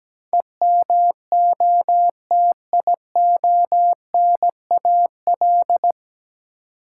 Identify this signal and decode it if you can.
Morse code, heard as a keyed tone, about 17 wpm, 700 Hz. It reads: EMOTIONAL
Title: Sentences from Top 1000 Words - Review Part 5 - New Word Drill - 17wpm